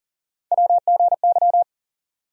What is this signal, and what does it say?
Morse code, heard as a keyed tone, about 40 wpm, 700 Hz. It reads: WGY
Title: USA Call Sign Postfixes - 3 Letters 40wpm